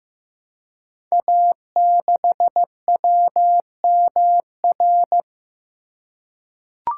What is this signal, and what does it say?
Morse code, heard as a keyed tone, about 15 wpm, 700 Hz. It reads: A6WMR
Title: Code Groups - 5 Letter:Numbers - 15wpm